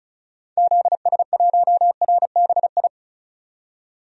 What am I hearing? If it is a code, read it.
Morse code, heard as a keyed tone, about 36 wpm, 700 Hz. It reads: ZS1RBI
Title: DXCC-100 Call Signs - 6 Characters - 35wpm